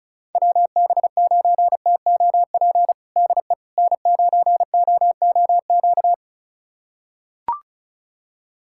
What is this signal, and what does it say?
Morse code, heard as a keyed tone, about 34 wpm, 700 Hz. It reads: WB9TOP DE N9OOQ
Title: USA Callsign DE Callsign 35wpm